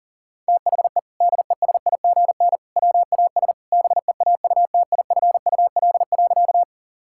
Morse code, heard as a keyed tone, about 37 wpm, 700 Hz. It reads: THE DESIGN WAS BEAUTIFUL.
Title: Sentences from Top 1000 Words - Review Part 5 - 40wpm